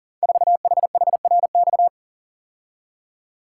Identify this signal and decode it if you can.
Morse code, heard as a keyed tone, about 41 wpm, 700 Hz. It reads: 4HHRX
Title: Code Groups - 5 Letter:Numbers - 40wpm